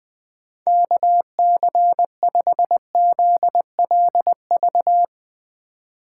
Morse code, heard as a keyed tone, about 20 wpm, 700 Hz. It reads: KC5ZLV
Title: USA Call Signs - 6 Characters - 3x Character Spacing - 20wpm